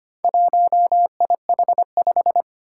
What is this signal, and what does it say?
Morse code, heard as a keyed tone, about 25 wpm, 700 Hz. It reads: 1IH5
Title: Four Challenging Letter-Number Combinations 25wpm